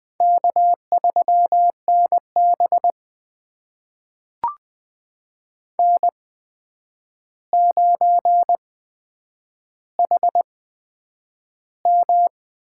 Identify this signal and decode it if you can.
Morse code, heard as a keyed tone, about 20 wpm, spaced as for 9 wpm, 700 Hz. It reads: K3NB N9HM
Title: USA Call Signs - 4 Characters - 8x Character Spacing - 20wpm